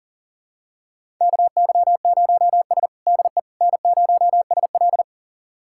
Morse code, heard as a keyed tone, about 40 wpm, 700 Hz. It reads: KY0S DE N0SL
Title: USA Callsign DE Callsign - 4 Characters - 40wpm